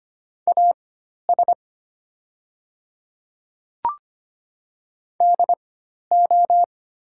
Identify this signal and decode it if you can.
Morse code, heard as a keyed tone, about 25 wpm, spaced as for 12 wpm, 690 Hz. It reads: AS DO
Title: 4x Character Spacing - Top 100 Words - 25wpm